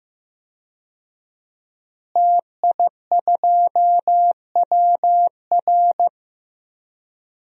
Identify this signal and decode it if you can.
Morse code, heard as a keyed tone, about 15 wpm, 700 Hz. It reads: TI2WR
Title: DXCC-100 Call Signs - 5 Characters - Repeated 3x - 15wpm